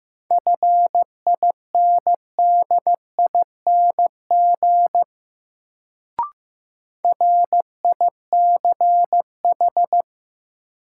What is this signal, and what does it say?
Morse code, heard as a keyed tone, about 15 wpm, 700 Hz. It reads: FINDING RICH
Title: Sentences from Top 1100 Words - New Word Drill - 15wpm